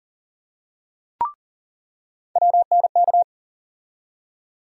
Morse code, heard as a keyed tone, about 40 wpm, 700 Hz. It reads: WNK
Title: USA Call Sign Postfixes - 3 Letters 40wpm